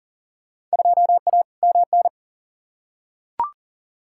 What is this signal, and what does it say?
Morse code, heard as a keyed tone, about 40 wpm, 700 Hz. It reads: 2A MN
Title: Field Day Exchanges - 40wpm